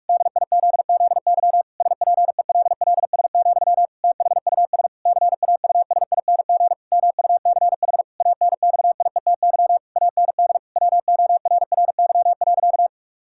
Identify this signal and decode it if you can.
Morse code, heard as a keyed tone, about 48 wpm, 700 Hz. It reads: DIZZY SPELLS, THUS CAUSING MUCH ANXIETY AND WORRY.
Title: Pangram Challenge v2 - 45wpm